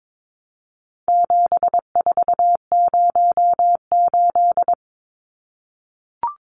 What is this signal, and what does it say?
Morse code, heard as a keyed tone, about 22 wpm, 700 Hz. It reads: 7408
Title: Numbers - 4-Digit 22wpm